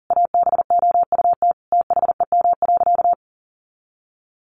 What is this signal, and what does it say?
Morse code, heard as a keyed tone, about 37 wpm, 700 Hz. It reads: ABOUT THEM.
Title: Sentences from Top 1500 Words 40wpm